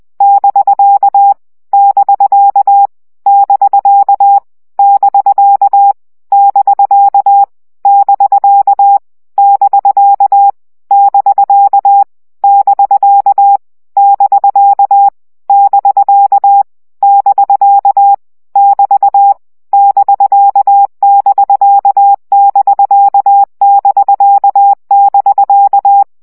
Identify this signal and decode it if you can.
Morse code, heard as a keyed tone, about 20 wpm, 800 Hz. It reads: <BK> <BK> <BK> <BK> <BK> <BK> <BK> <BK> <BK> <BK> <BK> <BK> <BT> <BK><BK><BK><BK><BK>
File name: CW gyakorlas bk 17 baud